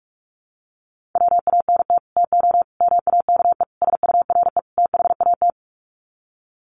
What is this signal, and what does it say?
Morse code, heard as a keyed tone, about 44 wpm, 700 Hz. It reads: WANT TO MAKE SURE THAT
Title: Sets of 5 Words 45wpm